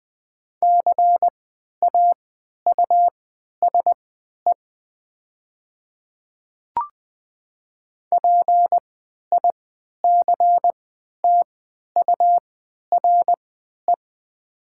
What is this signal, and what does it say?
Morse code, heard as a keyed tone, about 20 wpm, spaced as for 11 wpm, 700 Hz. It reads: CAUSE PICTURE
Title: 3x Character Spacing - Top 300 Words - 20wpm